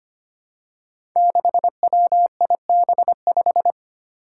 Morse code, heard as a keyed tone, about 25 wpm, 700 Hz. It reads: 6WIB5
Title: Five Challenging Letter-Number Combinations 25wpm